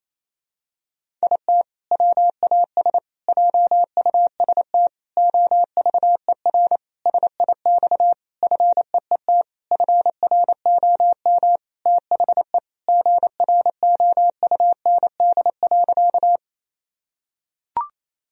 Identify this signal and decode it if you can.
Morse code, heard as a keyed tone, about 28 wpm, 700 Hz. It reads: IT WAS JUST OVER SIX FEET FROM THE GROUND.
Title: Sentences from Top 1000 Words - Review Part 2 - 28wpm